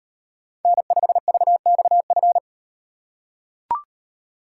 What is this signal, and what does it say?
Morse code, heard as a keyed tone, about 38 wpm, 700 Hz. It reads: N5VXF